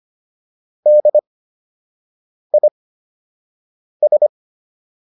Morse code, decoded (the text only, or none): DIS